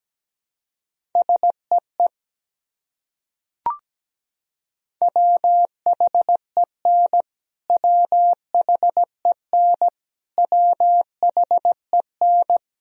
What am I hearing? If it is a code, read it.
Morse code, heard as a keyed tone, about 17 wpm, 700 Hz. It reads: SEE WHEN WHEN WHEN
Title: Top 100 Words Repeated 17wpm